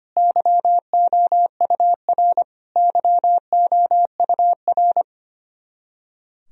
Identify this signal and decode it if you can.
Morse code, heard as a keyed tone, about 25 wpm, 700 Hz. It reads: YOUR YOUR
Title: Top 100 Words Repeated 25wpm